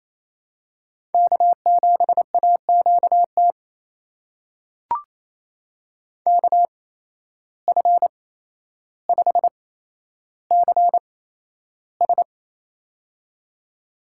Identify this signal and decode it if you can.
Morse code, heard as a keyed tone, about 28 wpm, spaced as for 12 wpm, 700 Hz. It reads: K7AQT KF5CS
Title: USA Call Signs - 8x Character Spacing - 28wpm